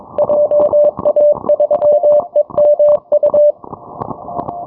Morse code, heard as a keyed tone, about 22 wpm, 590 Hz. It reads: JA3EMU
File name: onair